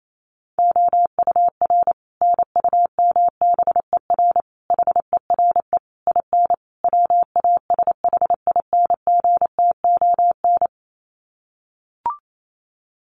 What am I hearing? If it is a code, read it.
Morse code, heard as a keyed tone, about 28 wpm, 700 Hz. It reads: OUR NUMBER HERE IN WASHINGTON
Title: Sets of 5 Words Encore2 28wpm